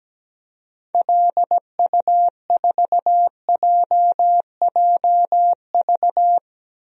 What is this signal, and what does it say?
Morse code, heard as a keyed tone, about 17 wpm, 700 Hz. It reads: LU4JJV
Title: DXCC-100 Call Signs - 6 Characters - Repeated 3x - 17wpm